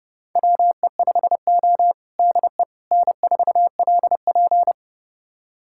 Morse code, heard as a keyed tone, about 30 wpm, 700 Hz. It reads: WE5O DE N4LP
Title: USA Callsign DE Callsign - 4 Characters - 30wpm